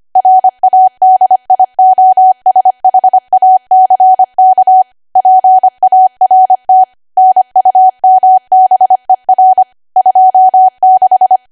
Morse code, heard as a keyed tone, about 25 wpm, 750 Hz. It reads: RADIOSHACK PART NUMBER 26
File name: cwa / W1AW-Level3 / 25WPM